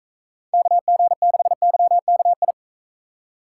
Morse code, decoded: KG6YKI

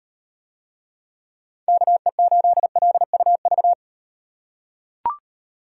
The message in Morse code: KE8LUV

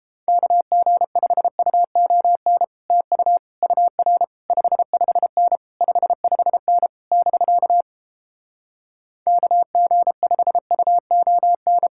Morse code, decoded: KG5UON TU UR 55N 55N <BK> KG5UON

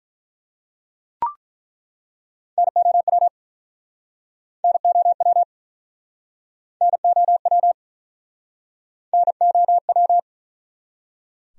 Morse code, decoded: NOW NOW NOW NOW

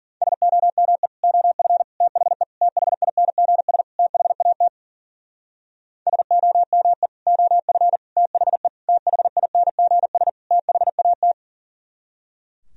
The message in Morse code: SOME OF THE THINGS THAT SOME OF THE THINGS THAT

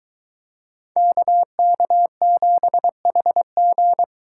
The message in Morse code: KK7HG

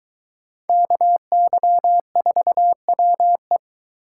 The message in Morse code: KY4WE

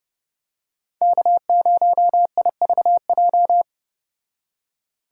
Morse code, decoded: K0IVJ

